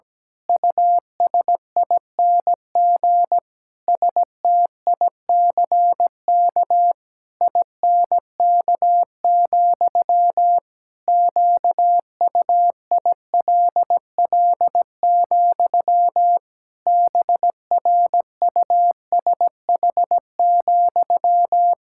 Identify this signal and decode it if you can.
Morse code, heard as a keyed tone, about 17 wpm, 700 Hz. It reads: USING STICK INK, QUILL, BRUSH,
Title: Pangram Challenge v2 - 17wpm